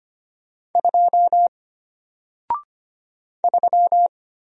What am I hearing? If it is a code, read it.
Morse code, heard as a keyed tone, about 25 wpm, 700 Hz. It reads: 2 3
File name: Single-Letter-Number-25wpm